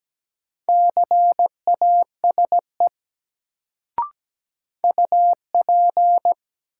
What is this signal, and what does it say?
Morse code, heard as a keyed tone, about 17 wpm, 700 Hz. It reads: CASE UP
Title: Sets of 2 Words - New Word Drill - 17wpm